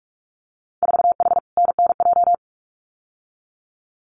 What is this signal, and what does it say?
Morse code, heard as a keyed tone, about 46 wpm, 700 Hz. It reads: 4H NNJ
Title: Winter Field Day Exchanges Encore 45wpm